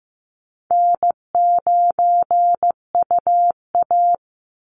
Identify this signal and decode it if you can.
Morse code, heard as a keyed tone, about 15 wpm, 700 Hz. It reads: N9UA
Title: Easy Call Signs Repeated 2x - 15wpm